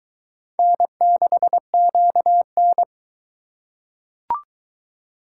Morse code, heard as a keyed tone, about 22 wpm, 700 Hz. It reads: N6QN